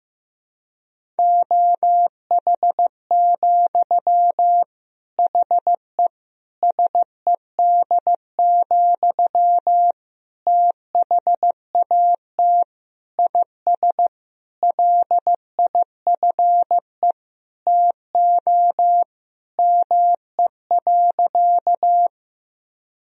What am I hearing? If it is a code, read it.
Morse code, heard as a keyed tone, about 15 wpm, 700 Hz. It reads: OH, HE SED, THAT IS LIFE TO ME.
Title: Sentences from Top 500 Words and Abbreviations 15wpm